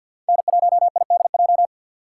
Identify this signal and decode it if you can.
Morse code, heard as a keyed tone, about 52 wpm, 700 Hz. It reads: N1IDJ
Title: USA Call Signs - 5 Characters - 3x Character Spacing - 50wpm